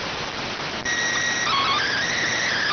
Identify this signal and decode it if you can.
SSTV (slow-scan television), PD 290.